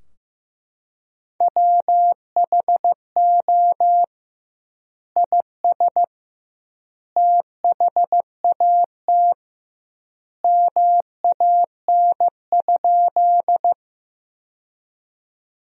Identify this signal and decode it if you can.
Morse code, heard as a keyed tone, about 15 wpm, 700 Hz. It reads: WHO IS THAT MAN?
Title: 2x Word-Spacing - Sentences from Top 200 Words - 15wpm